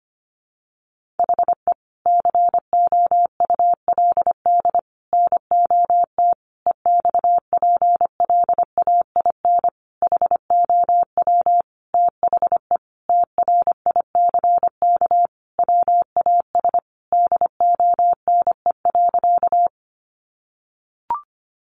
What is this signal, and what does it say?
Morse code, heard as a keyed tone, about 25 wpm, 700 Hz. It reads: HE COULD NOT EXPLAIN HOW THE TRICK WAS DONE.
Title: Sentences from Top 1500 Words 25wpm